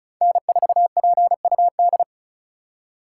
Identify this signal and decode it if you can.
Morse code, heard as a keyed tone, about 36 wpm, 700 Hz. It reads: N4PUD
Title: USA Call Signs - 2x Character Spacing - 35wpm